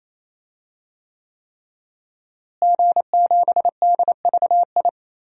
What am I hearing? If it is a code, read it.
Morse code, heard as a keyed tone, about 28 wpm, 700 Hz. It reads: G7DVI
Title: DXCC-100 Call Signs - 5 Characters - Repeated 3x - 28wpm